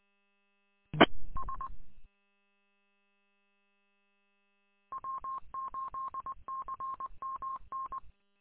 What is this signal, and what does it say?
Morse code, heard as a keyed tone, about 20 wpm, 1070 Hz. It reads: E S W8CMN
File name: w8cmn- -combined